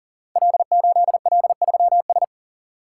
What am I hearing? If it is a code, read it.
Morse code, heard as a keyed tone, about 42 wpm, 700 Hz. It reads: L8L3S
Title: Code Groups - 5 Letter:Numbers - 40wpm